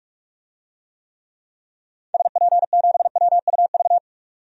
Morse code, heard as a keyed tone, about 46 wpm, 700 Hz. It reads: SP7WUV
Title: DXCC-100 Call Signs - 6 Characters - 45wpm